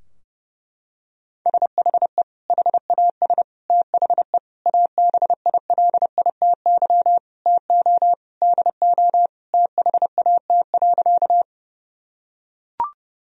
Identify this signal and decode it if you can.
Morse code, heard as a keyed tone, about 30 wpm, 700 Hz. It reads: SHE HAS THE ABILITY TO DO THAT.
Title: Sentences from Top 1000 Words - Review Part 3 - 30wpm